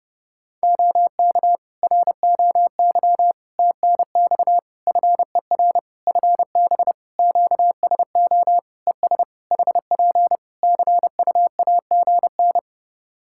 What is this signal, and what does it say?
Morse code, heard as a keyed tone, about 30 wpm, 700 Hz. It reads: OK ROY TNX FER FB QSO ES HP CUAGN
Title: Practice QSO - Exchange 4 - Ending - 30wpm